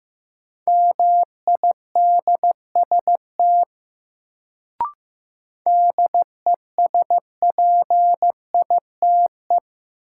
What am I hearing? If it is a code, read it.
Morse code, heard as a keyed tone, about 15 wpm, 700 Hz. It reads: MIDST DESPITE
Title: Sets of 4 Words - New Word Drill - 15wpm